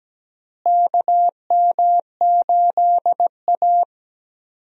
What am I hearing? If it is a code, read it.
Morse code, heard as a keyed tone, about 17 wpm, 700 Hz. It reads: KM8A
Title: USA Call Signs - 4 Characters - Repeated 3x - 17wpm